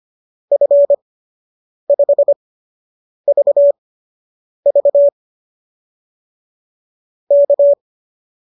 Morse code, heard as a keyed tone, about 25 wpm, spaced as for 8 wpm, 570 Hz. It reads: F5VV K